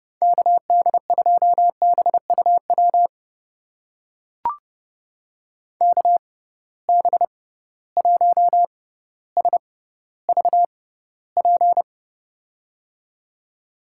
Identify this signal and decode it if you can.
Morse code, heard as a keyed tone, about 30 wpm, spaced as for 15 wpm, 700 Hz. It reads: KD2BUW KB1SVP
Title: USA Call Signs - 6 Characters - 6x Character Spacing - 30wpm